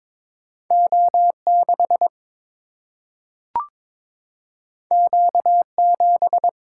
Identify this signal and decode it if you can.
Morse code, heard as a keyed tone, about 22 wpm, 700 Hz. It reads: O6 Q7
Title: Single-Letter Single-Number 22wpm